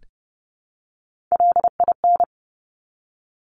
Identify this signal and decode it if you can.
Morse code, heard as a keyed tone, about 30 wpm, 700 Hz. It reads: LIN
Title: Top Three Letter Combinations in Words 30wpm